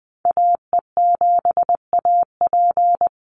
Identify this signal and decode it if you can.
Morse code, heard as a keyed tone, about 20 wpm, 700 Hz. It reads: AE7AP
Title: NA SOTA Activator Callsigns 20wpm